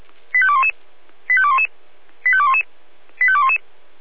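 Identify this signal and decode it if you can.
Five-tone selective call (ZVEI), heard as quick runs of five tones, tones of about 70 ms. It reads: 85211 85211 85211 85211
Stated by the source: ZVEI five-tone selective call received by SDR.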